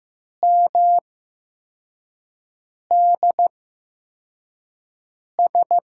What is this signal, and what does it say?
Morse code, heard as a keyed tone, about 15 wpm, spaced as for 4 wpm, 700 Hz. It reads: MDS
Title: USA Call Signs - 5 Characters - 8x Character Spacing - 15wpm